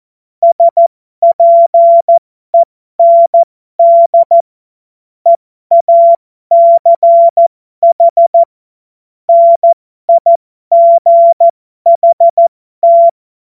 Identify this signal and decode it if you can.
Morse code, heard as a keyed tone, about 14 wpm, spaced as for 12 wpm, 680 Hz. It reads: SPEND EACH NIGHT
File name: Morsekurs / Witze